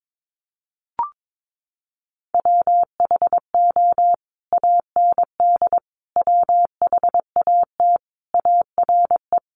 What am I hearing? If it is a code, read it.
Morse code, heard as a keyed tone, about 22 wpm, 700 Hz. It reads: WHO AND WHAT ARE
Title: Sentences from Top 100 Words 22wpm